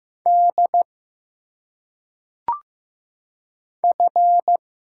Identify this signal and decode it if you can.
Morse code, heard as a keyed tone, about 15 wpm, 700 Hz. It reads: D F
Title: Single Letters - Phonetic Pronunciation 15wpm